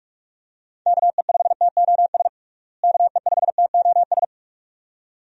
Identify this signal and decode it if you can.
Morse code, heard as a keyed tone, about 46 wpm, 700 Hz. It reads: KE5TOS KE5TOS
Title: USA Call Signs - 6 Characters - Repeated 2x - 45wpm